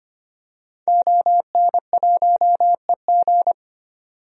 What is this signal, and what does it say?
Morse code, heard as a keyed tone, about 25 wpm, 700 Hz. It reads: ON1EG
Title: DXCC-100 Call Signs - 5 Characters - 25wpm